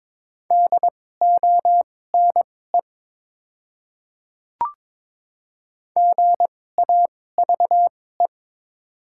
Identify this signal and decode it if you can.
Morse code, heard as a keyed tone, about 22 wpm, spaced as for 16 wpm, 700 Hz. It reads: DONE GAVE
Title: 2x Character Spacing - Top 500 Words - 22wpm